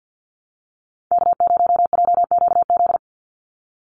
Morse code, heard as a keyed tone, about 50 wpm, 700 Hz. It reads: K0JQZ
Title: NA SOTA Activator Callsigns 50wpm